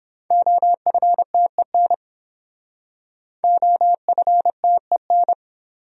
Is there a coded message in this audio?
Yes: Morse code from 0.3 to 5.3 s, heard as a keyed tone, about 32 wpm, 700 Hz. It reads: OFTEN OFTEN